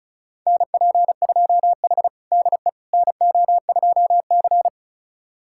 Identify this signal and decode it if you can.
Morse code, heard as a keyed tone, about 36 wpm, 700 Hz. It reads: NP2H DE NO2C
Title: USA Callsign DE Callsign - 4 Characters - 35wpm